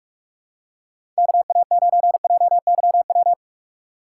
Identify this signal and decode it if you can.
Morse code, heard as a keyed tone, about 43 wpm, 700 Hz. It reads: KA9JYW
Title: USA Call Signs - 6x Character Spacing - 45wpm